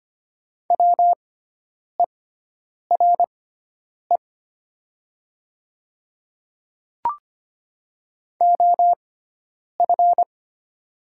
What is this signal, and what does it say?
Morse code, heard as a keyed tone, about 25 wpm, spaced as for 9 wpm, 700 Hz. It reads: WERE OF
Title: 6x Character Spacing - Top 100 Words - 25wpm